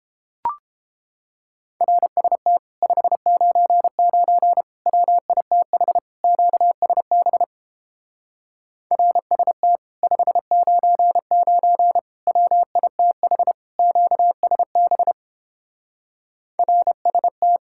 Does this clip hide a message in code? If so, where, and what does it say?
Yes: Morse code from 1.8 to 17.6 s, heard as a keyed tone, about 29 wpm, 700 Hz. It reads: RST 599 WITH QSB RST 599 WITH QSB RST